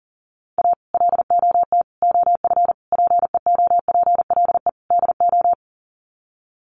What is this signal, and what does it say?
Morse code, heard as a keyed tone, about 40 wpm, 700 Hz. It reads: A LOT OF PEOPLE DO
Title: Sets of 5 Words 40wpm